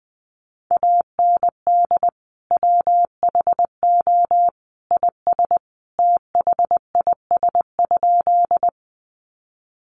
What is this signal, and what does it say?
Morse code, heard as a keyed tone, about 19 wpm, 700 Hz. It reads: AND WHO IS THIS?
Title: Sentences from Top 100 Words 20wpm